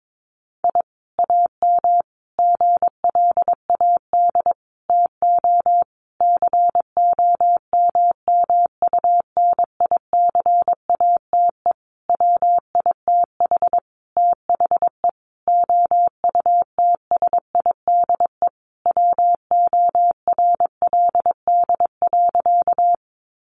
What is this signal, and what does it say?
Morse code, heard as a keyed tone, about 22 wpm, 700 Hz. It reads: I AM GLAD TO COMMUNICATE WITH THE OUTSIDE WORLD.
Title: Sentences from Top 1300 Words 22wpm